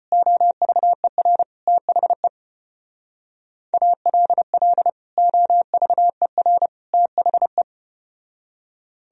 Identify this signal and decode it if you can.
Morse code, heard as a keyed tone, about 33 wpm, 700 Hz. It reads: OVER THE ALL OVER THE